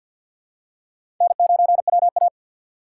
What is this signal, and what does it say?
Morse code, heard as a keyed tone, about 52 wpm, 700 Hz. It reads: N9WA
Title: USA Call Signs - 4 Characters - 3x Character Spacing - 50wpm